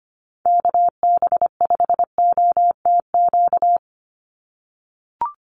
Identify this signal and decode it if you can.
Morse code, heard as a keyed tone, about 25 wpm, 700 Hz. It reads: KB5OTQ
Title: USA Call Signs 25wpm